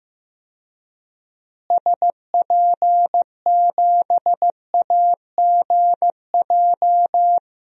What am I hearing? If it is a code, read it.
Morse code, heard as a keyed tone, about 15 wpm, 700 Hz. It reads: SP7AGJ
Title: DXCC-100 Call Signs - 6 Characters - Repeated 2x - 15wpm